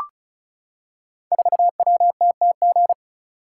Morse code, heard as a keyed tone, about 36 wpm, 700 Hz. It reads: 4WTTG
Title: Code Groups - 5 Letter:Numbers - 35wpm